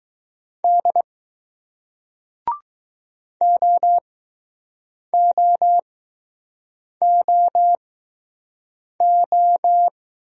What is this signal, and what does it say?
Morse code, heard as a keyed tone, about 20 wpm, spaced as for 6 wpm, 700 Hz. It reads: D OOOO